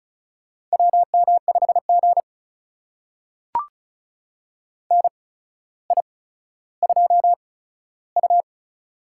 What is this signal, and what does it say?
Morse code, heard as a keyed tone, about 36 wpm, spaced as for 15 wpm, 700 Hz. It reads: WM5G NI2U
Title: USA Call Signs - 4 Characters - 8x Character Spacing - 35wpm